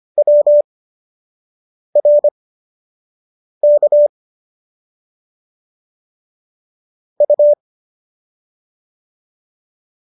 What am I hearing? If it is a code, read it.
Morse code, heard as a keyed tone, about 25 wpm, spaced as for 6 wpm, 590 Hz. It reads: WRK U